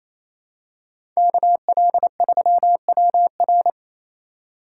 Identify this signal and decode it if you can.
Morse code, heard as a keyed tone, about 28 wpm, 700 Hz. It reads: KL3WR
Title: USA Call Signs - 4x Character Spacing - 28wpm